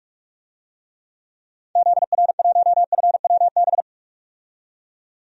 Morse code, heard as a keyed tone, about 42 wpm, 700 Hz. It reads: ZR1FWB
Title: DXCC-100 Call Signs - 6 Characters - 45wpm